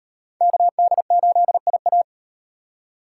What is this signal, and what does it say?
Morse code, heard as a keyed tone, about 36 wpm, 700 Hz. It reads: KD8IA